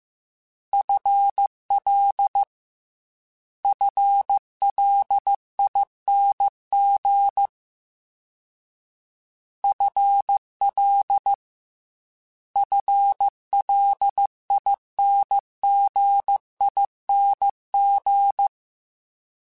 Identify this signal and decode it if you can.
Morse code, heard as a keyed tone, about 15 wpm, 800 Hz. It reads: FL FLING FL FLINGING
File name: FL / FL mp3s / FL Prefixes 1x1 @13